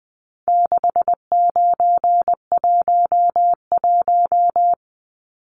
Morse code, heard as a keyed tone, about 20 wpm, 700 Hz. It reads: 6911
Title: Numbers - 4-Digit 20wpm